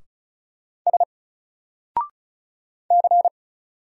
Morse code, heard as a keyed tone, about 35 wpm, 700 Hz. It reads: S C
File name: Single-Letter-Number-35wpm